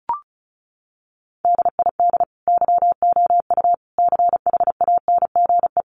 Morse code, heard as a keyed tone, about 34 wpm, 700 Hz. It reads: DID YOU CHANGE
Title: Sentences from Top 1400 Words 35wpm